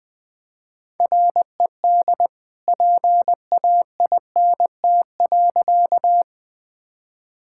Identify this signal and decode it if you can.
Morse code, heard as a keyed tone, about 21 wpm, 700 Hz. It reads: RED PAINT.
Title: All-Age Appropriate Jokes - 20wpm